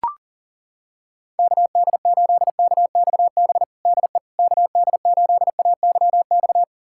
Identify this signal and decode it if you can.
Morse code, heard as a keyed tone, about 40 wpm, 700 Hz. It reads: KD8KXB DE KD8AYX
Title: USA Callsign DE Callsign 40wpm